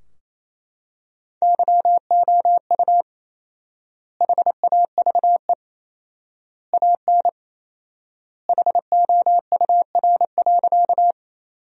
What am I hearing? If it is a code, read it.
Morse code, heard as a keyed tone, about 28 wpm, 700 Hz. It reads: YOU HAVE AN HOUR.
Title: 4x Word-Spacing - Sentences from Top 400 Words - 28wpm